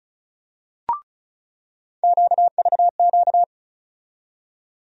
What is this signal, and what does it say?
Morse code, heard as a keyed tone, about 36 wpm, 700 Hz. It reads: QVQ
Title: USA Call Sign Postfixes - 3 Letters 35wpm